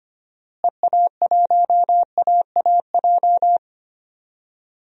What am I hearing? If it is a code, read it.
Morse code, heard as a keyed tone, about 25 wpm, 700 Hz. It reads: EA1AAJ
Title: DXCC-100 Call Signs - 6 Characters - Repeated 2x - 25wpm